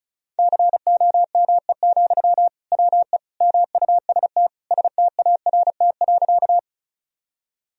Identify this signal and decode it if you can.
Morse code, heard as a keyed tone, about 34 wpm, 700 Hz. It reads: COME, WE MUST START.
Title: Sentences from Top 1000 Words - Review Part 1 - 35wpm